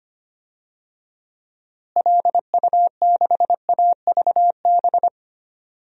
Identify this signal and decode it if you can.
Morse code, heard as a keyed tone, about 25 wpm, 700 Hz. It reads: LU6AVB
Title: DXCC-100 Call Signs - 6 Characters - Repeated 3x - 25wpm